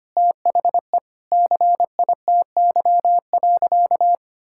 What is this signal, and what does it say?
Morse code, heard as a keyed tone, about 25 wpm, 700 Hz. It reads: THE CITY.
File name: practice / 25wpm / Sentences with 3-Word Phrases - 25wpm